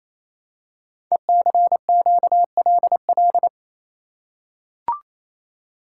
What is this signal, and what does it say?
Morse code, heard as a keyed tone, about 27 wpm, 700 Hz. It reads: ECQLL
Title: Code Groups - 5 Letters - 28wpm